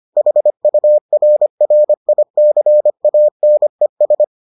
Morse code, heard as a keyed tone, about 25 wpm, 600 Hz. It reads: HURRICANES